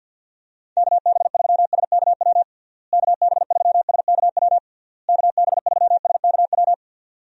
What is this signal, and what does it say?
Morse code, heard as a keyed tone, about 51 wpm, 700 Hz. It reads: KB3SKW KB3SKW KB3SKW
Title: USA Call Signs - 6 Characters - Repeated 3x - 50wpm